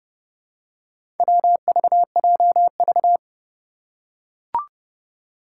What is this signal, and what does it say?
Morse code, heard as a keyed tone, about 30 wpm, 700 Hz. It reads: WVJV
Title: Four Challenging Letter-Number Combinations 30wpm